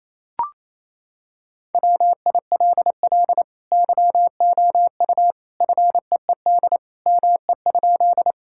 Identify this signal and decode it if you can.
Morse code, heard as a keyed tone, about 28 wpm, 700 Hz. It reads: WILL YOU FEED ME?
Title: Sentences from Top 1600 Words 28wpm